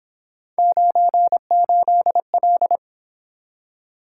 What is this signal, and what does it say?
Morse code, heard as a keyed tone, about 28 wpm, 700 Hz. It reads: 98L